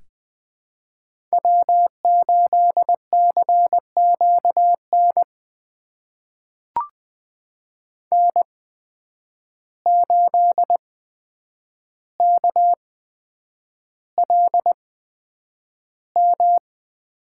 Morse code, heard as a keyed tone, about 20 wpm, spaced as for 9 wpm, 700 Hz. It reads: W8CQN N8KLM
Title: USA Call Signs - 5 Characters - 8x Character Spacing - 20wpm